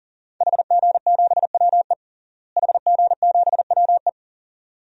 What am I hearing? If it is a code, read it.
Morse code, heard as a keyed tone, about 40 wpm, 700 Hz. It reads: HG7WE HG7WE
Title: DXCC-100 Call Signs - 5 Characters - Repeated 2x - 40wpm